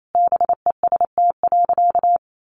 Morse code, heard as a keyed tone, about 28 wpm, 700 Hz. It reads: BEST.
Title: Sentences from Top 1000 Words 28wpm